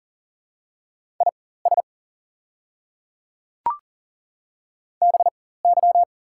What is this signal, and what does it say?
Morse code, heard as a keyed tone, about 40 wpm, spaced as for 19 wpm, 700 Hz. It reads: IS BY
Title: 4x Character Spacing - Top 100 Words - 40wpm